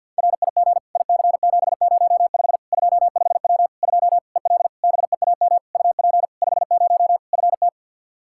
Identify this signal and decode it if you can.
Morse code, heard as a keyed tone, about 52 wpm, 700 Hz. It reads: RIG IC705 25W 2 EL BEAM UP 50 FT
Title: Practice QSO - Exchange 2 - Info - 50wpm